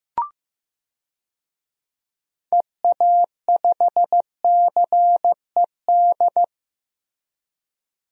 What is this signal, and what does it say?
Morse code, heard as a keyed tone, about 15 wpm, 700 Hz. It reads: EA5CED
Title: DXCC-100 Call Signs - 6 Characters - Repeated 3x - 15wpm